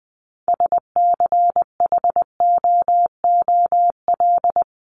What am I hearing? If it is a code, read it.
Morse code, heard as a keyed tone, about 20 wpm, 700 Hz. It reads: SCHOOL
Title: Top 300 Words 20wpm